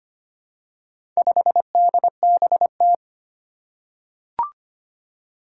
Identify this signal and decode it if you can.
Morse code, heard as a keyed tone, about 25 wpm, 700 Hz. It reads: 5DBT
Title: Four Challenging Letter-Number Combinations 25wpm